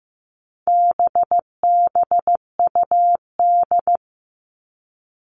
Morse code, heard as a keyed tone, about 15 wpm, 700 Hz. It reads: BBUD
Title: All Cut Numbers - 4-Digit 15wpm